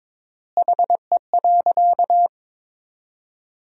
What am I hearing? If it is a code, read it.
Morse code, heard as a keyed tone, about 22 wpm, 700 Hz. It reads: HE.
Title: 3x Word-Spacing - Sentences from Top 100 Words - 22wpm